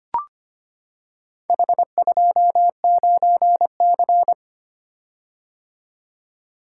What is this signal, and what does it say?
Morse code, heard as a keyed tone, about 25 wpm, 700 Hz. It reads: H29C
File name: practice / 25wpm / Four Letter-Number Combinations - 25wpm